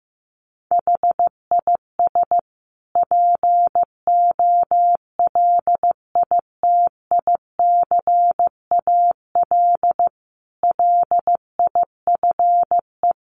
Morse code, heard as a keyed tone, about 15 wpm, 700 Hz. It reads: HIS POLITICAL LIFE
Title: Sentences from Top 1500 Words 15wpm